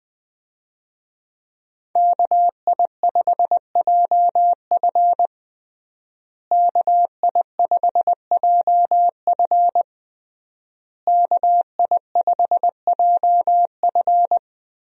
Morse code, heard as a keyed tone, about 20 wpm, 700 Hz. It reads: KI5JF KI5JF KI5JF
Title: USA Call Signs - 5 Characters - Repeated 3x - 20wpm